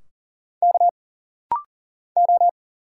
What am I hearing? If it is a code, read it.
Morse code, heard as a keyed tone, about 40 wpm, 700 Hz. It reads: K O
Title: Single Letters - Rapid-Fire 40wpm